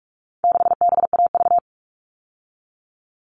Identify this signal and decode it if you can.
Morse code, heard as a keyed tone, about 44 wpm, 700 Hz. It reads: 6BAV
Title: All Cut Numbers - 4-Digit 45wpm